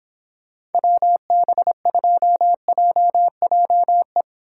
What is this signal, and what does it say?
Morse code, heard as a keyed tone, about 31 wpm, 700 Hz. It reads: WB2JJE